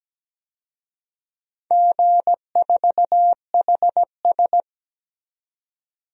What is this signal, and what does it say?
Morse code, heard as a keyed tone, about 17 wpm, 700 Hz. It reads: G4HS
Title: DXCC-100 Call Signs - 4 Characters - 17wpm